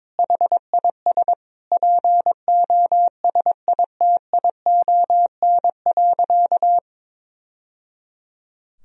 Morse code, heard as a keyed tone, about 22 wpm, 700 Hz. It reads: HIS POSITION.
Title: Sentences from Top 1000 Words - Review Part 1 - 22wpm